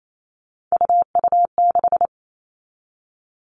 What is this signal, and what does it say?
Morse code, heard as a keyed tone, about 28 wpm, 700 Hz. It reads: UU6
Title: All Cut Numbers - 3-Digit 28wpm